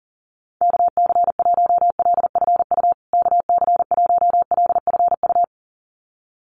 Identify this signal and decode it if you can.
Morse code, heard as a keyed tone, about 40 wpm, 700 Hz. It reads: KC1LFU KC1LFU
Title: USA Call Signs Repeated 2x 40wpm